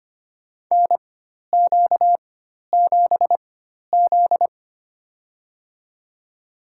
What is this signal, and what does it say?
Morse code, heard as a keyed tone, about 25 wpm, spaced as for 12 wpm, 700 Hz. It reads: NQ7Z
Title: USA Call Signs - 4x Character Spacing - 25wpm